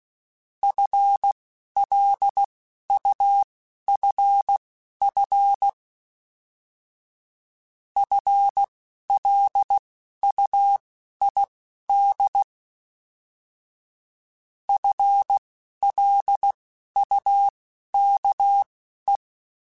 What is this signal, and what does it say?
Morse code, heard as a keyed tone, about 16 wpm, spaced as for 11 wpm, 800 Hz. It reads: FLUFF FLUID FLUKE